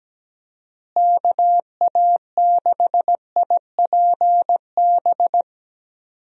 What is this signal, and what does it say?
Morse code, heard as a keyed tone, about 17 wpm, 700 Hz. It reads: KA6IPB
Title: USA Call Signs - 6 Characters - Repeated 2x - 17wpm